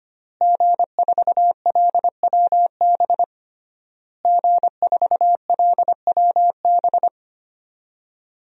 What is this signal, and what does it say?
Morse code, heard as a keyed tone, about 25 wpm, 700 Hz. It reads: G4LWB G4LWB
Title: DXCC-100 Call Signs - 5 Characters - Repeated 3x - 25wpm